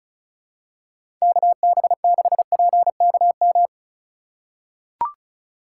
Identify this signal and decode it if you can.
Morse code, heard as a keyed tone, about 36 wpm, 700 Hz. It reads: KB6PKM
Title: USA Call Signs - 6 Characters - 3x Character Spacing - 35wpm